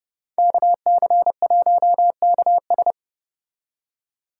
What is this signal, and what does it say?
Morse code, heard as a keyed tone, about 30 wpm, 700 Hz. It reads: KC1KS